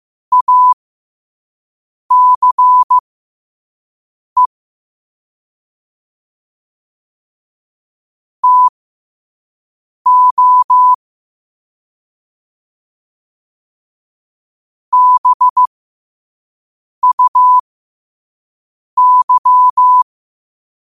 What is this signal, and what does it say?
Morse code, heard as a keyed tone, about 15 wpm, spaced as for 5 wpm, 1000 Hz. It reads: ACE TO BUY
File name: qotdfiles / 5wpm